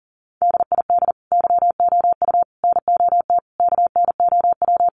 Morse code, heard as a keyed tone, about 40 wpm, 700 Hz. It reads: DID YOU NOT KNOW
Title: Sentences from Top 600 Words 40wpm